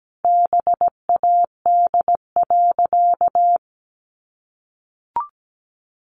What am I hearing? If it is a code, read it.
Morse code, heard as a keyed tone, about 17 wpm, 700 Hz. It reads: BAD.
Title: Sentences from Top 700 Words 17wpm